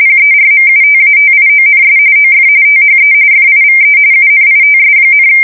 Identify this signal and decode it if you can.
RTTY, about 45.45 baud. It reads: PERMUTATIONS OF THE FIVE UNITS,
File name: fsk170 / printing-telegraph